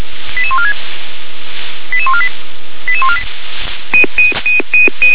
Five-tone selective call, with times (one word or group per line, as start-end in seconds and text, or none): 0.4-0.7 88146
1.9-2.3 88148
2.9-3.2 88148